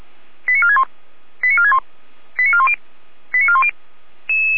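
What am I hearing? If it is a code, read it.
Five-tone selective call (ZVEI), heard as quick runs of five tones, tones of about 70 ms. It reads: 89461 89461 89410 89410
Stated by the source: ZVEI five-tone selective call 89170, received by SDR.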